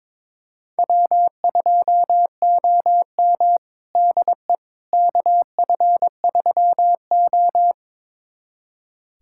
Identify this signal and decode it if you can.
Morse code, heard as a keyed tone, about 21 wpm, 700 Hz. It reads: W2OM DE KF3O
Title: USA Callsign DE Callsign - 4 Characters - 22wpm